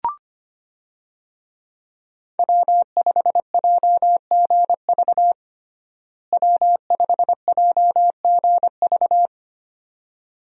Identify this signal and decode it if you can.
Morse code, heard as a keyed tone, about 25 wpm, 700 Hz. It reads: W5JGV W5JGV
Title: USA Call Signs - 5 Characters - Repeated 2x - 25wpm